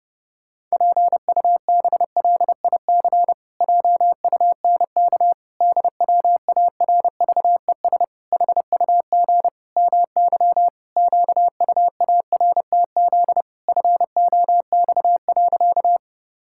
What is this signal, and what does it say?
Morse code, heard as a keyed tone, about 30 wpm, 700 Hz. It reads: PUBLIC JUNK DWARVES HUG MY QUARTZ FOX.
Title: Pangram Challenge v2 - 30wpm